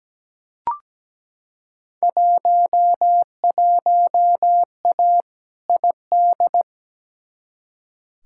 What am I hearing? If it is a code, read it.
Morse code, heard as a keyed tone, about 17 wpm, 700 Hz. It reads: 11A ID
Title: Field Day Exchanges Encore - 17wpm